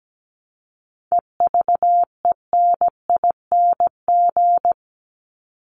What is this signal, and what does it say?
Morse code, heard as a keyed tone, about 17 wpm, 700 Hz. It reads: EVENING
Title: Top 1501-1750 Words 17wpm